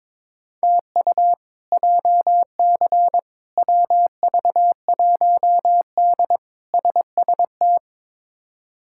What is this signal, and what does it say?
Morse code, heard as a keyed tone, about 22 wpm, 700 Hz. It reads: TU JC WV1D SST
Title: K1USN SST QSO Practice - 22wpm